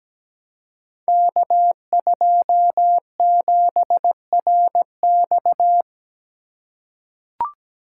Morse code, heard as a keyed tone, about 17 wpm, 700 Hz. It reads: K27RX
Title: Code Groups - 5 Letter:Numbers - 17wpm